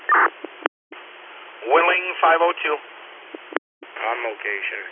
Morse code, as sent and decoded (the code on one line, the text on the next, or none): none